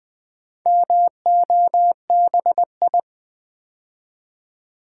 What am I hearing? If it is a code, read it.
Morse code, heard as a keyed tone, about 20 wpm, 700 Hz. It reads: MOBI
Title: Build A Word - 2 Letters At A Time - 8-Letter Words - 20wpm